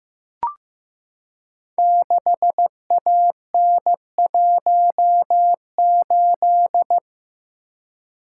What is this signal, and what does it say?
Morse code, heard as a keyed tone, about 15 wpm, 700 Hz. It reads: 6AN18
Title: Code Groups - 5 Letter:Numbers - 15wpm